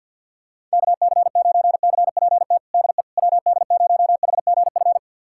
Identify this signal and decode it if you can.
Morse code, heard as a keyed tone, about 48 wpm, 700 Hz. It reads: KC9CPT DE WD0HGF
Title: USA Callsign DE Callsign - 6 Characters - 50wpm